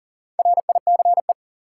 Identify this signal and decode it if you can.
Morse code, heard as a keyed tone, about 40 wpm, 700 Hz. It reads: RICE
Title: Four Letter Words - 40wpm